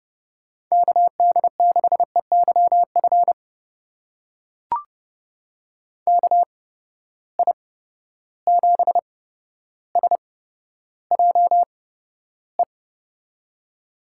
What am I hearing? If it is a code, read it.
Morse code, heard as a keyed tone, about 30 wpm, spaced as for 13 wpm, 700 Hz. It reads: KD6EYF KI7SJE
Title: USA Call Signs - 6 Characters - 8x Character Spacing - 30wpm